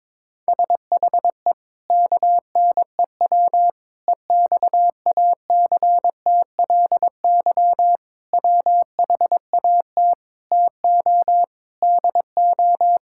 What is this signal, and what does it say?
Morse code, heard as a keyed tone, about 22 wpm, 700 Hz. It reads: SHE KNEW EXACTLY WHAT TO DO
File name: practice / 22wpm / Sentences with 8-Word Phrases - 22wpm